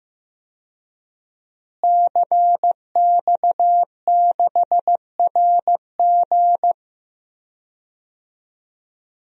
Morse code, heard as a keyed tone, about 15 wpm, 700 Hz. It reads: CX6RG
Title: DXCC-100 Call Signs - 5 Characters - Repeated 2x - 15wpm